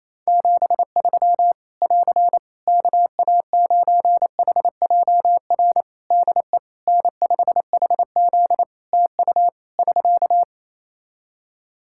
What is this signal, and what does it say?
Morse code, heard as a keyed tone, about 28 wpm, 700 Hz. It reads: 73 <AR> KA9HJR DE N5HZ TU <SK>
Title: Practice QSO - Exchange 4 - Ending - 28wpm